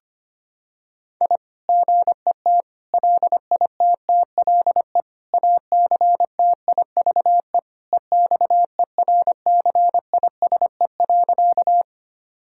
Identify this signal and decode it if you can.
Morse code, heard as a keyed tone, about 25 wpm, 700 Hz. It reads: I GET LITTLE ACTIVE EXERCISE.
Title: Sentences from Top 1000 Words - Review Part 6 - 25wpm